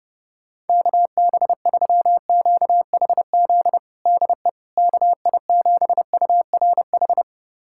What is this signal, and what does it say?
Morse code, heard as a keyed tone, about 32 wpm, 700 Hz. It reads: KB3QHZ DE KI7URH
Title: USA Callsign DE Callsign - 6 Characters - 30wpm